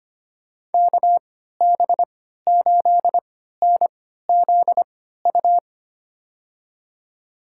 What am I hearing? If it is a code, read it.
Morse code, heard as a keyed tone, about 25 wpm, spaced as for 14 wpm, 700 Hz. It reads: KB8NZU
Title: USA Call Signs - 6 Characters - 3x Character Spacing - 25wpm